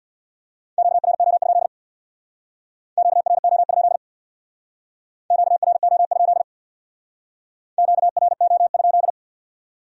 Morse code, heard as a keyed tone, about 62 wpm, 700 Hz. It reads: QRO? QRO? QRO? QRO?
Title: Speed-Racing Q-Codes - 50wpm